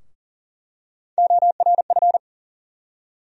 Morse code, heard as a keyed tone, about 40 wpm, 700 Hz. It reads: ORF
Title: USA Call Sign Postfixes - 3 Letters 40wpm